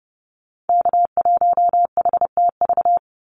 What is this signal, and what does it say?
Morse code, heard as a keyed tone, about 30 wpm, 700 Hz. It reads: K1HTV
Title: Call Signs - 30wpm